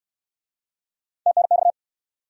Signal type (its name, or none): Morse code